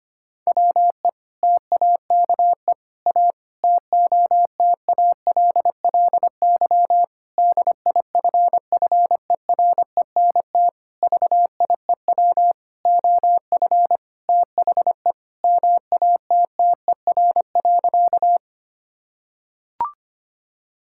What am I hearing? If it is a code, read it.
Morse code, heard as a keyed tone, about 25 wpm, 700 Hz. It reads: WE TAKE A TOTALLY DIFFERENT VIEW OF THE MATTER.
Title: Sentences from Top 1600 Words 25wpm